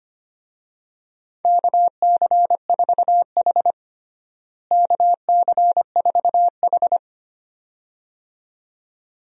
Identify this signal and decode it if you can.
Morse code, heard as a keyed tone, about 25 wpm, 700 Hz. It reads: KC4H KC4H
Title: USA Call Signs - 4 Characters - Repeated 2x - 25wpm